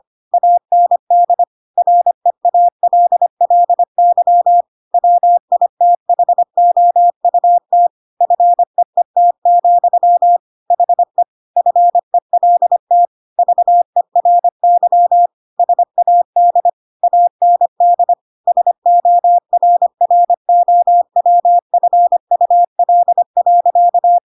Morse code, heard as a keyed tone, about 25 wpm, 700 Hz. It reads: AND REALLY WITHOUT FEET, HE FELT VERY SAD AND SORROWFUL.
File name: Pinocchio / Pinocchio ch7 25 25 ik1hge